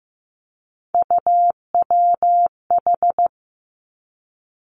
Morse code, heard as a keyed tone, about 15 wpm, 700 Hz. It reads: UWH
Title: Three Challenging Letter-Number Combinations 15wpm